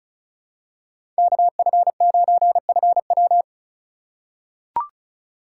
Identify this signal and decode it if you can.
Morse code, heard as a keyed tone, about 34 wpm, 700 Hz. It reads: KF9FW
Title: USA Call Signs - 5 Characters - 4x Character Spacing - 35wpm